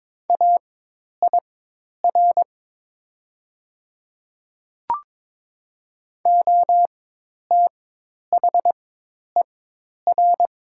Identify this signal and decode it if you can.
Morse code, heard as a keyed tone, about 22 wpm, spaced as for 10 wpm, 700 Hz. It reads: AIR OTHER
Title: 4x Character Spacing - Top 300 Words - 22wpm